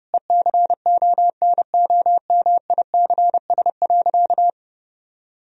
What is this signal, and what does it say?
Morse code, heard as a keyed tone, about 30 wpm, 700 Hz. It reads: ECONOMICS.
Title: Sentences from Top 1000 Words - Review Part 3 - 30wpm